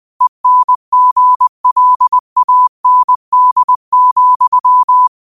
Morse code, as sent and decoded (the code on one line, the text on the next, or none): . -. --. .-.. .- -. -.. --..--
ENGLAND,